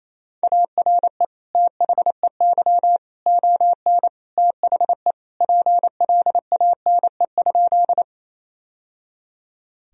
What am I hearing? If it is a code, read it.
Morse code, heard as a keyed tone, about 28 wpm, 700 Hz. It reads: ARE THEY ON THE PLANE?
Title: Sentences from Top 1000 Words - Review Part 3 - 28wpm